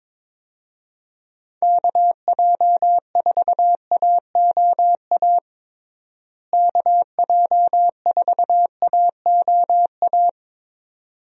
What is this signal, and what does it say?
Morse code, heard as a keyed tone, about 22 wpm, 700 Hz. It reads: KJ4AOA KJ4AOA
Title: USA Call Signs - 6 Characters - Repeated 2x - 22wpm